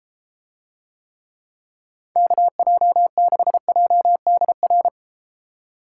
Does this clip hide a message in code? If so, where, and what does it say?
Yes: Morse code from 2.2 to 4.9 s, heard as a keyed tone, about 32 wpm, 700 Hz. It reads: KJ6JDR